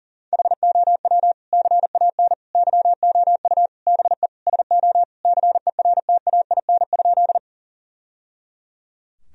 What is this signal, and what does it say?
Morse code, heard as a keyed tone, about 40 wpm, 700 Hz. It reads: HOW CAN YOU BE SO CERTAIN?
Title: Sentences from Top 1000 Words - Review Part 2 - 40wpm